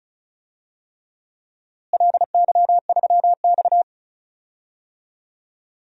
Morse code, heard as a keyed tone, about 34 wpm, 700 Hz. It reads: LY3X
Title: DXCC-100 Call Signs - 4 Characters - Repeated 3x - 35wpm